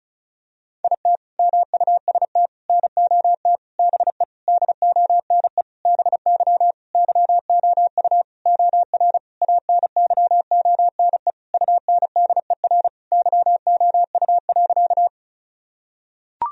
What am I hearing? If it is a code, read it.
Morse code, heard as a keyed tone, about 36 wpm, 700 Hz. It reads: IT MUST NOT BE DONE BY YOU OR ANYONE UNDER YOU.
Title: Sentences from Top 1000 Words - Review Part 1 - 35wpm